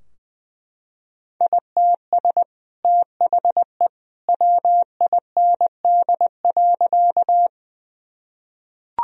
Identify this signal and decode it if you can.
Morse code, heard as a keyed tone, about 20 wpm, 700 Hz. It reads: ITS THE WIND.
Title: Sentences from Top 1000 Words - Review Part 2 - 20wpm